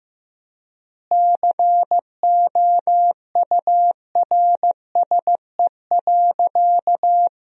Morse code, heard as a keyed tone, about 15 wpm, 700 Hz. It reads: COURSE.
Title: 4x Word-Spacing - Sentences from Top 500 Words - 15wpm